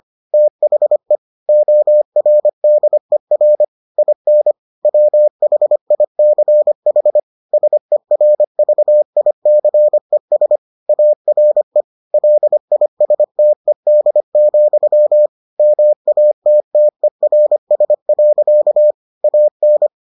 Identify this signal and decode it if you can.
Morse code, heard as a keyed tone, about 25 wpm, 600 Hz. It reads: THE ORDER IN WHICH SERVICES ARE LISTED, MATTERS. AN